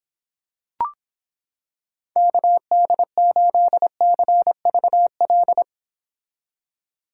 Morse code, heard as a keyed tone, about 26 wpm, 700 Hz. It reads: KD8CVL